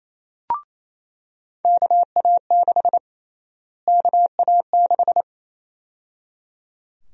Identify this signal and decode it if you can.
Morse code, heard as a keyed tone, about 27 wpm, 700 Hz. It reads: KA6 KA6
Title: USA Call Sign Prefixes - Repeated 2x - 28wpm